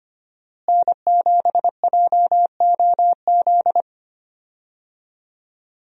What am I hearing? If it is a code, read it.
Morse code, heard as a keyed tone, about 25 wpm, 700 Hz. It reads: N7JOZ
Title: USA Call Signs - 5 Characters - Repeated 2x - 25wpm